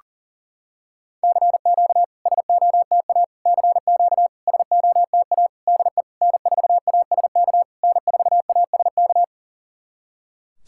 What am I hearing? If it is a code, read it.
Morse code, heard as a keyed tone, about 40 wpm, 690 Hz. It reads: CQ SOTA CQ SOTA DE N4ASK N4ASK
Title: SOTA QSO Practice 40wpm